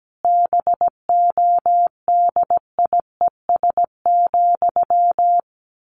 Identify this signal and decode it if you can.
Morse code, heard as a keyed tone, about 17 wpm, 700 Hz. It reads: BODIES,
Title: A Sound Of Thunder 17wpm